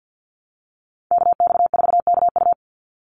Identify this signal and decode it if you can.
Morse code, heard as a keyed tone, about 50 wpm, 700 Hz. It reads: KX4KU
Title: Call Signs Repeated 2x - 50wpm